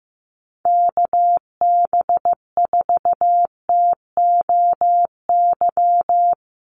Morse code, heard as a keyed tone, about 15 wpm, 700 Hz. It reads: KB4TOY
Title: USA Call Signs Repeated 2x 15wpm